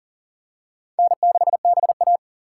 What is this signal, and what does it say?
Morse code, heard as a keyed tone, about 37 wpm, 690 Hz. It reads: N6BA
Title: USA Call Signs - 4 Characters - 2x Character Spacing - 40wpm